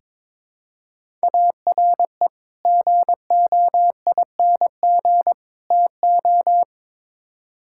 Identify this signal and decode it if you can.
Morse code, heard as a keyed tone, about 22 wpm, 700 Hz. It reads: ARE GOING TO
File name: practice / 22wpm / Build A Phrase - 5-Word Phrases - 22wpm